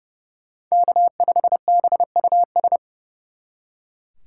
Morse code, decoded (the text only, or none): K5BUS